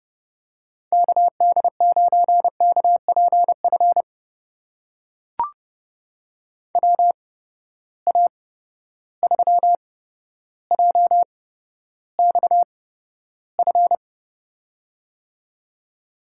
Morse code, heard as a keyed tone, about 32 wpm, spaced as for 13 wpm, 700 Hz. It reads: KD9KPF WA3JXF
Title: USA Call Signs - 6 Characters - 8x Character Spacing - 30wpm